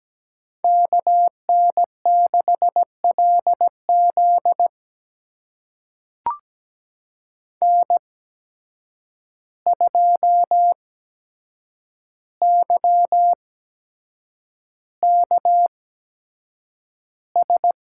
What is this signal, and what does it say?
Morse code, heard as a keyed tone, about 17 wpm, spaced as for 7 wpm, 700 Hz. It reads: KN6LZ N2YKS
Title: USA Call Signs - 5 Characters - 8x Character Spacing - 17wpm